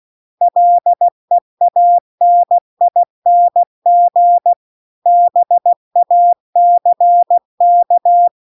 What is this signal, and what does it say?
Morse code, heard as a keyed tone, about 16 wpm, 700 Hz. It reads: LEANING BACK
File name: CW-The-Big-Twoheaded-River-2-16WPM